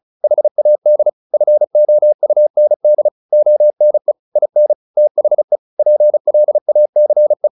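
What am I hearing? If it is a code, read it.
Morse code, heard as a keyed tone, about 34 wpm, 600 Hz. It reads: HAD FOUND ONE IN THE PLACE
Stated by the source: Morse practice at 35 wpm from CW med Gnister, sending Ken Follett's novel Never in English.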